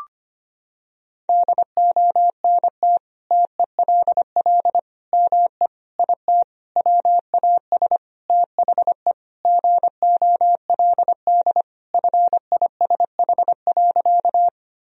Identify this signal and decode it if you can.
Morse code, heard as a keyed tone, about 25 wpm, 700 Hz. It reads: DONT TELL ME IT WAS THE GOLD FISH.
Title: Sentences from Top 1000 Words - Review Part 3 - 25wpm